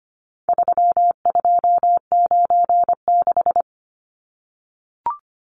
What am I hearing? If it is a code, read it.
Morse code, heard as a keyed tone, about 25 wpm, 700 Hz. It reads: 3296
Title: Numbers - 4-Digit 25wpm